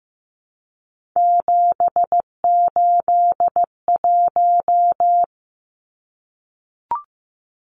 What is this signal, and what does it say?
Morse code, heard as a keyed tone, about 15 wpm, 700 Hz. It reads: 781
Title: Numbers - 3-Digit 15wpm